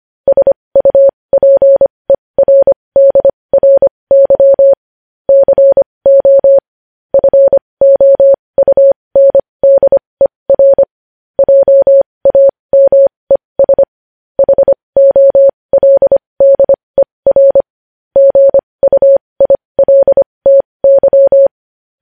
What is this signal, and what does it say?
Morse code, heard as a keyed tone, about 25 wpm, spaced as for 20 wpm, 550 Hz. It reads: SUPERDRY CO FOUNDER JAMES HOLDER GUILTY
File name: podcast / News headlines-1x-25x20